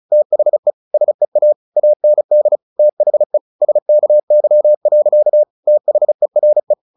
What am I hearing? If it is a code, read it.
Morse code, heard as a keyed tone, about 34 wpm, 600 Hz. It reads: THE SEA AND THE SKY. THERE
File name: the old man and the sea 2 35wpm